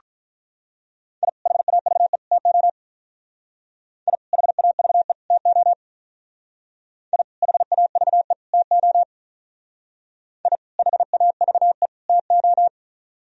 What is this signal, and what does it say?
Morse code, heard as a keyed tone, about 43 wpm, 700 Hz. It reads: I HAVE TO I HAVE TO I HAVE TO I HAVE TO
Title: Speed-Racing - Sets of 3 Words 35wpm